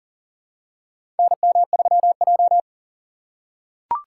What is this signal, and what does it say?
Morse code, heard as a keyed tone, about 40 wpm, 700 Hz. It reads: NM3J
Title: USA Call Signs - 2x Character Spacing - 40wpm